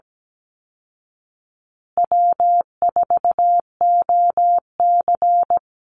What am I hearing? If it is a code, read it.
Morse code, heard as a keyed tone, about 17 wpm, 700 Hz. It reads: W4OC
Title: Easy Call Signs Repeated 3x - 17wpm